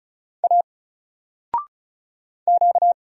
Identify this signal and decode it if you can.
Morse code, heard as a keyed tone, about 36 wpm, 700 Hz. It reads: A Q
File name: Single-Letter-Number-35wpm